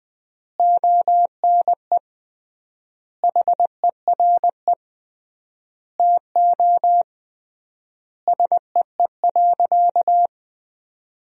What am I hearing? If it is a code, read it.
Morse code, heard as a keyed tone, about 20 wpm, 700 Hz. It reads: ONE HERE TO SEE.
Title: 3x Word-Spacing - Sentences from Top 200 Words - 20wpm